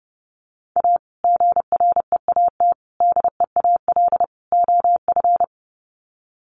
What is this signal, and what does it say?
Morse code, heard as a keyed tone, about 30 wpm, 700 Hz. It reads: A GREAT DEAL OF
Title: Sets of 4 Words 30wpm